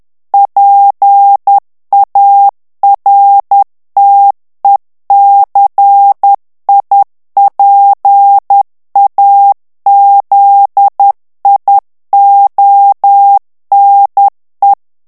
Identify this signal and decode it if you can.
Morse code, heard as a keyed tone, about 10 wpm, 800 Hz. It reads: PARTECIPAZIONE
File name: ITALIEN texte clair 2 vitesse 10 mots minute